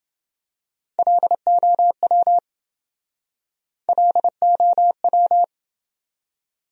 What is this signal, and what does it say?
Morse code, heard as a keyed tone, about 26 wpm, 700 Hz. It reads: LOW LOW